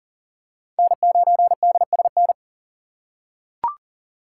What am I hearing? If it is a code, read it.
Morse code, heard as a keyed tone, about 38 wpm, 700 Hz. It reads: N9DSN
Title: USA Call Signs - 5 Characters - 3x Character Spacing - 40wpm